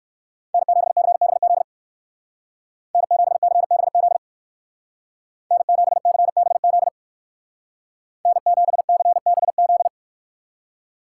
Morse code, heard as a keyed tone, about 53 wpm, 700 Hz. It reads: N7CBZ N7CBZ N7CBZ N7CBZ
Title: Speed Racing - USA Call Signs 45wpm